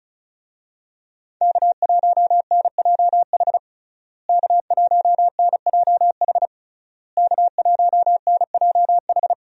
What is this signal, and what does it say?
Morse code, heard as a keyed tone, about 36 wpm, 700 Hz. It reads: K1NJH K1NJH K1NJH
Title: USA Call Signs - 5 Characters - Repeated 3x - 35wpm